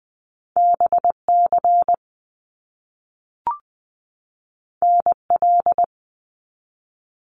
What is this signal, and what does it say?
Morse code, heard as a keyed tone, about 20 wpm, 700 Hz. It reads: BC NL
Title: Canadian Provinces and Territories 20wpm